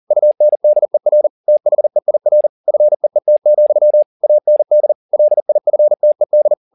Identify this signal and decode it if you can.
Morse code, heard as a keyed tone, about 40 wpm, 600 Hz. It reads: UNDER THEIR FEET, AND LIFTED